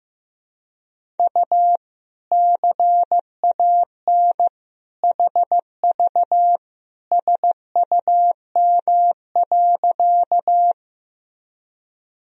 Morse code, heard as a keyed tone, about 15 wpm, 700 Hz. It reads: U CAN HV SUM.
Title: Sentences from Top 100 Words and Abbreviations 15wpm